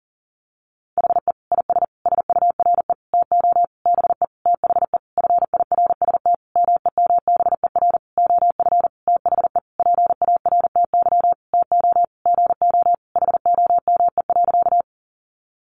Morse code, heard as a keyed tone, about 40 wpm, 700 Hz. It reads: HE IS SURE TO BE THE FIRST MEMBER OF THE PARTY TO GO HOME.
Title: Sentences from Top 1400 Words 40wpm